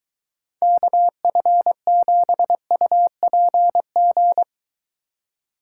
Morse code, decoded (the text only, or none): KF7UPG